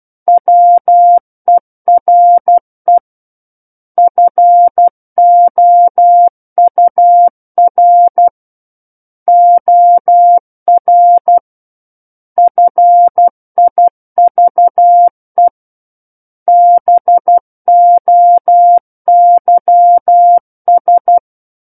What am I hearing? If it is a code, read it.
Morse code, heard as a keyed tone, about 12 wpm, 700 Hz. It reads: WERE FOUR OR FIVE BOYS
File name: Book One / 03-Chapter 3 - 12 WPM